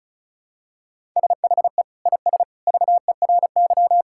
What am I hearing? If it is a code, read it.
Morse code, heard as a keyed tone, about 34 wpm, 700 Hz. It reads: SHE IS VERY
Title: Sentences from Top 1000 Words - Review Part 6 - 35wpm